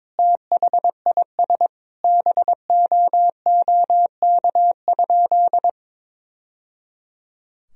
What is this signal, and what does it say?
Morse code, heard as a keyed tone, about 21 wpm, 700 Hz. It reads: THIS BOOK?
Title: Sentences from Top 1000 Words - Review Part 4 - 22wpm